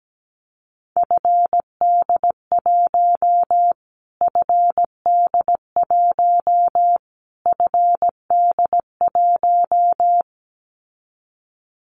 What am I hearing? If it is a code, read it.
Morse code, heard as a keyed tone, about 17 wpm, 700 Hz. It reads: FD1 FD1 FD1
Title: Callsign Prefixes for Top 25 DXCC - Repeated 3x - 17wpm